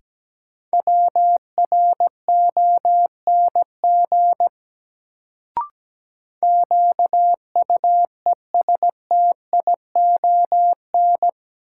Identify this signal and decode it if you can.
Morse code, heard as a keyed tone, about 17 wpm, 700 Hz. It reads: WRONG QUESTION
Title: Sets of 5 Words Encore - New Word Drill - 17wpm